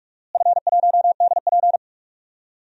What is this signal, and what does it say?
Morse code, heard as a keyed tone, about 41 wpm, 700 Hz. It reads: F1DP
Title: DXCC-100 Call Signs - 4 Characters - Repeated 2x - 45wpm